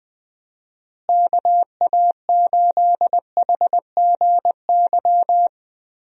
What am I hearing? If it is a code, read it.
Morse code, heard as a keyed tone, about 20 wpm, 700 Hz. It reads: KA8HGY
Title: USA Call Signs - 6 Characters - Repeated 2x - 20wpm